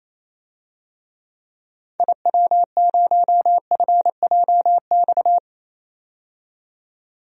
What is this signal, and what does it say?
Morse code, heard as a keyed tone, about 28 wpm, 700 Hz. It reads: IW0FJX
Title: DXCC-100 Call Signs - 6 Characters - 28wpm